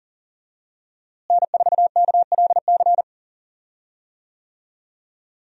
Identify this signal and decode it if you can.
Morse code, heard as a keyed tone, about 37 wpm, 700 Hz. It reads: N4KLC
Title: USA Call Signs - 5 Characters - Repeated 2x - 40wpm